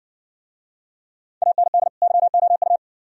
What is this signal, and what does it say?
Morse code, heard as a keyed tone, about 60 wpm, 700 Hz. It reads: AND YOU